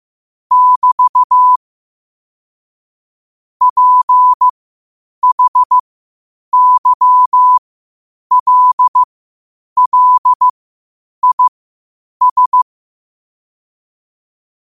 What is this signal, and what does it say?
Morse code, heard as a keyed tone, about 15 wpm, spaced as for 8 wpm, 1000 Hz. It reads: <BT> PHYLLIS
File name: qotdfiles / 8wpm